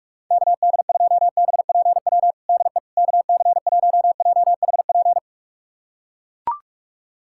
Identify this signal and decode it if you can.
Morse code, heard as a keyed tone, about 42 wpm, 700 Hz. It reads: KD2BPW DE KC1JHP
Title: USA Callsign DE Callsign 45wpm